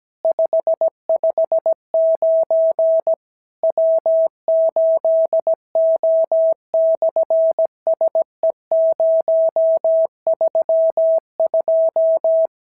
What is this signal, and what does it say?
Morse code, heard as a keyed tone, about 17 wpm, 650 Hz. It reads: 559 W8O/SE032
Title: SOTA QSO Practice 17wpm